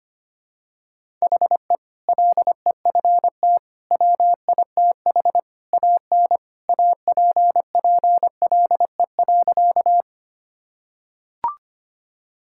Morse code, heard as a keyed tone, about 25 wpm, 700 Hz. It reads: HE LEFT WITH AN APPLE.
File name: practice / 25wpm / Sentences with 2-Word Phrases - 25wpm